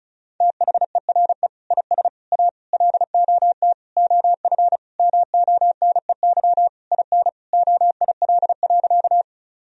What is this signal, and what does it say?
Morse code, heard as a keyed tone, about 36 wpm, 700 Hz. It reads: THERE IS A LOT OF MONEY IN OIL.
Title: Sentences from Top 1000 Words - Review Part 4 - 35wpm